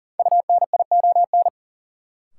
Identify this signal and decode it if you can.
Morse code, heard as a keyed tone, about 40 wpm, 700 Hz. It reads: UNION
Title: Sets of 5 Words Encore - New Word Drill - 40wpm